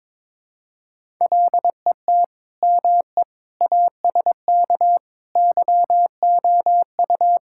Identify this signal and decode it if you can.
Morse code, heard as a keyed tone, about 22 wpm, 700 Hz. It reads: LET ME ASK YOU
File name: practice / 22wpm / Sentences with 4-Word Phrases - 22wpm